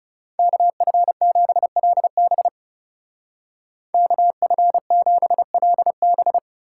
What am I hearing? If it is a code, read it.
Morse code, heard as a keyed tone, about 37 wpm, 700 Hz. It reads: KF7LB KF7LB
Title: Speed Racing - USA 2x2 Call Signs 30wpm